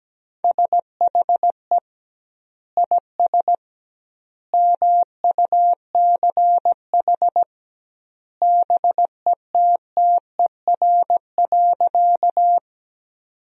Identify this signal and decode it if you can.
Morse code, heard as a keyed tone, about 17 wpm, 700 Hz. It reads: SHE IS MUCH BETTER.
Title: 2x Word-Spacing - Sentences from Top 400 Words - 17wpm